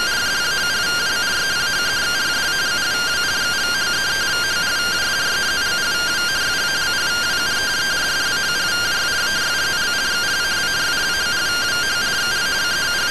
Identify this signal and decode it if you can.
Two-tone teleprinter signal, about 170 Hz shift.